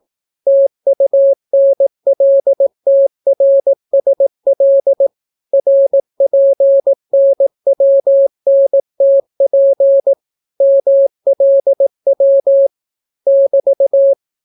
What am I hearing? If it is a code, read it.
Morse code, heard as a keyed tone, about 18 wpm, 550 Hz. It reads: TUNLTRSL RPNWNTP MLW <BT>